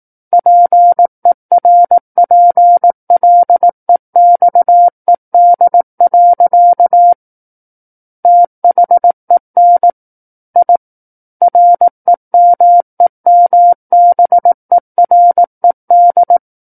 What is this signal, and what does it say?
Morse code, heard as a keyed tone, about 18 wpm, 700 Hz. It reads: PERPLEXED. THEN I REMEMBERED
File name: 18 WPM